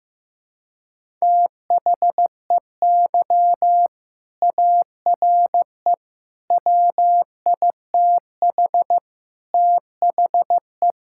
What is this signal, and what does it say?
Morse code, heard as a keyed tone, about 15 wpm, 700 Hz. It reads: THEY ARE WITH THE
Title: Sentences from Top 1000 Words - Review Part 5 - 15wpm